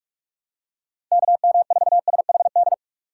Morse code, heard as a keyed tone, about 46 wpm, 700 Hz. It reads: KM4SHD
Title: USA Call Signs - 6 Characters - 3x Character Spacing - 45wpm